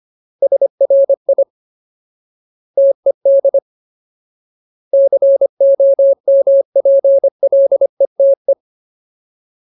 Morse code, heard as a keyed tone, about 25 wpm, 550 Hz. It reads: SRI TED COMPLETE
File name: Ultimate Word List / 25WPM Ultimate